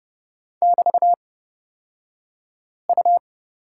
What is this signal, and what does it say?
Morse code, heard as a keyed tone, about 30 wpm, spaced as for 5 wpm, 700 Hz. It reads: <BT>U
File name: Lesson 36d - BT - Two Characters 30wpm